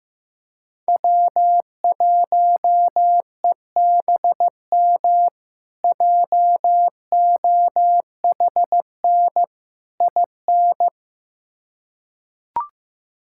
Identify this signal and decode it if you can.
Morse code, heard as a keyed tone, about 15 wpm, 700 Hz. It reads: W1EBM JOHN IN
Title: K1USN SST QSO Practice - 15wpm